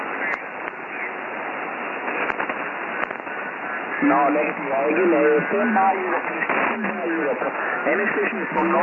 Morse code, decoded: none